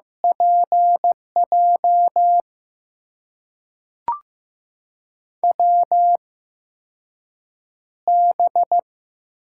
Morse code, heard as a keyed tone, about 15 wpm, spaced as for 6 wpm, 700 Hz. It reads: PJ WB
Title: USA Call Signs - 6 Characters - 8x Character Spacing - 15wpm